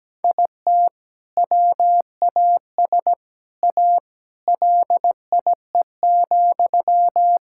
Morse code, heard as a keyed tone, about 17 wpm, 700 Hz. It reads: IT WAS A LIE,
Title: Sentences from Top 1600 Words 17wpm